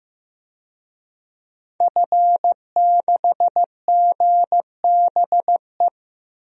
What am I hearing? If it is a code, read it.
Morse code, heard as a keyed tone, about 15 wpm, 700 Hz. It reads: F6GBE
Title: DXCC-100 Call Signs - 5 Characters - 15wpm